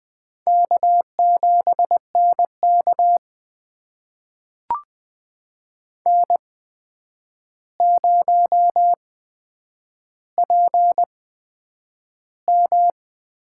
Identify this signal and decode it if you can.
Morse code, heard as a keyed tone, about 20 wpm, spaced as for 9 wpm, 700 Hz. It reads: K7NK N0PM
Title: USA Call Signs - 4 Characters - 8x Character Spacing - 20wpm